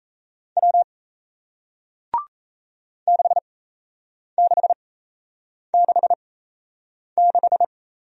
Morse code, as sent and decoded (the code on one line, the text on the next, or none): .-- / -.... -.... -.... -....
W 6666